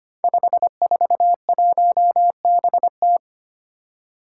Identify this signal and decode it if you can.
Morse code, heard as a keyed tone, about 25 wpm, 700 Hz. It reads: 541BT
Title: Five Challenging Letter-Number Combinations 25wpm